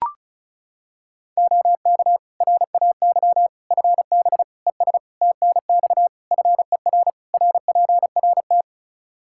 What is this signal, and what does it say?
Morse code, heard as a keyed tone, about 36 wpm, 700 Hz. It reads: OK RAY FB ES TNX FER RPRT
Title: Practice QSO - Exchange 2 - Info - 35wpm